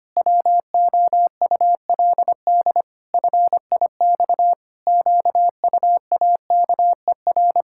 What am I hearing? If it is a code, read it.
Morse code, heard as a keyed tone, about 25 wpm, 700 Hz. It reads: WOULD FIX QUAKER
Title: Pangram Challenge v2 - 25wpm